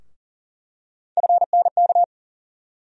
Morse code, heard as a keyed tone, about 40 wpm, 700 Hz. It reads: FNK
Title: USA Call Sign Postfixes - 3 Letters 40wpm